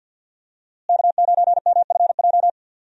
Morse code, heard as a keyed tone, about 47 wpm, 700 Hz. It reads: K9MFJ
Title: USA Call Signs - 5 Characters - Repeated 3x - 50wpm